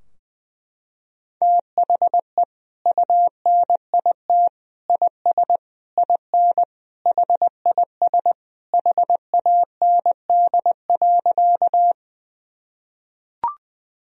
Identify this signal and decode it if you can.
Morse code, heard as a keyed tone, about 20 wpm, 700 Hz. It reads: THE UNIT IS IN HIS HAND.
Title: Sentences from Top 1000 Words - Review Part 2 - 20wpm